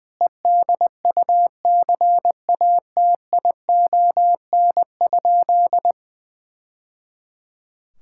Morse code, heard as a keyed tone, about 20 wpm, 700 Hz. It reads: EDUCATION?
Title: Sentences from Top 1000 Words - Review Part 6 - 20wpm